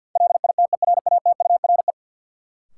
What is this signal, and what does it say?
Morse code, heard as a keyed tone, about 52 wpm, 700 Hz. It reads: LITERATURE
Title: Sentences from Top 600 Words - New Word Drill - 50wpm